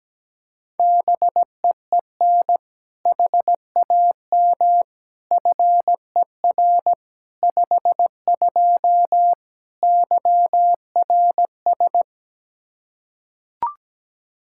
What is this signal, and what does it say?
Morse code, heard as a keyed tone, about 17 wpm, 700 Hz. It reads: BEEN HAM FER 52 YRS
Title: Practice QSO - Exchange 3 - More Info - 17wpm